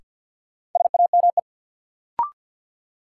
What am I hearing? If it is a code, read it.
Morse code, heard as a keyed tone, about 53 wpm, 700 Hz. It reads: SAME
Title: Sentences from Top 200 Words - New Word Drill - 50wpm